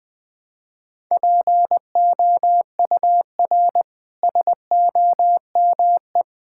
Morse code, heard as a keyed tone, about 21 wpm, 700 Hz. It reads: POUR SOME
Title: Easy Sentences with Hints - 20wpm